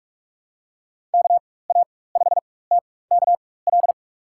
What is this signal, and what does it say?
Morse code, heard as a keyed tone, about 46 wpm, spaced as for 21 wpm, 700 Hz. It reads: KA5TKL
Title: USA Call Signs - 6 Characters - 4x Character Spacing - 45wpm